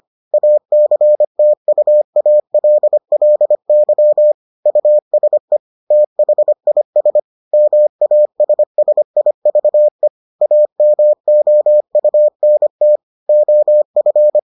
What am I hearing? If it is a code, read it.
Morse code, heard as a keyed tone, about 25 wpm, 600 Hz. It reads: ACTUALLY USE THIS MASSIVE AMOUNT OF